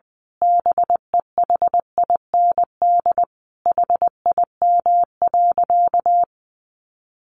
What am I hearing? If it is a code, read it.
Morse code, heard as a keyed tone, about 20 wpm, 700 Hz. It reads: BEHIND HIM.
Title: Sentences from Top 1000 Words 20wpm